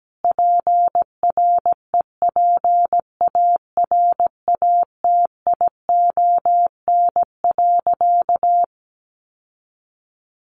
Morse code, heard as a keyed tone, about 17 wpm, 700 Hz. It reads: PREPARATION.
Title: Sentences from Top 1400 Words 17wpm